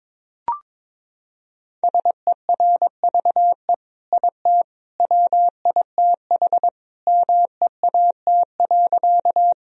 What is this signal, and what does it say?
Morse code, heard as a keyed tone, about 22 wpm, 700 Hz. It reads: SERVE IT WITH MEAT.
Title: Sentences from Top 1000 Words - Review Part 3 - 22wpm